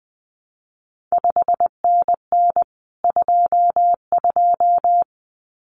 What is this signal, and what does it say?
Morse code, heard as a keyed tone, about 20 wpm, 700 Hz. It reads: 5NN 22
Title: CQ WW DX Contest Exchange Encore - 20wpm